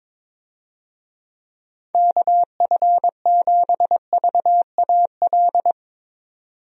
Morse code, heard as a keyed tone, about 21 wpm, 700 Hz. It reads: KF7VAL
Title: USA Call Signs - 6 Characters - 22wpm